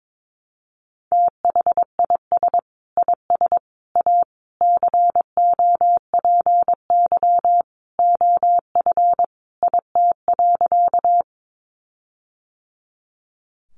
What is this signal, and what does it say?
Morse code, heard as a keyed tone, about 22 wpm, 700 Hz. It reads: THIS IS A COPY OF IT.
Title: Sentences from Top 1400 Words 22wpm